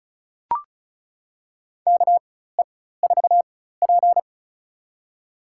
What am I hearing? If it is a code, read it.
Morse code, heard as a keyed tone, about 34 wpm, spaced as for 16 wpm, 700 Hz. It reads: KE4P
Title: USA Call Signs - 4 Characters - 4x Character Spacing - 35wpm